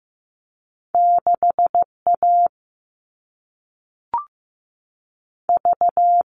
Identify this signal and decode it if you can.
Morse code, heard as a keyed tone, about 15 wpm, spaced as for 3 wpm, 700 Hz. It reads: 6 AV
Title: All Cut Numbers - 2-Digit 15wpm